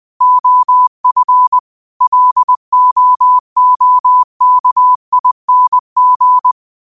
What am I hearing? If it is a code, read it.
Morse code, heard as a keyed tone, about 20 wpm, 1000 Hz. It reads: OF LOOKING
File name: qotdfiles / 20wpm